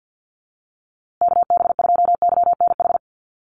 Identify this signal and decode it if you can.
Morse code, heard as a keyed tone, about 50 wpm, 700 Hz. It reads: KB2YNH
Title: USA Call Signs Repeated 3x 50wpm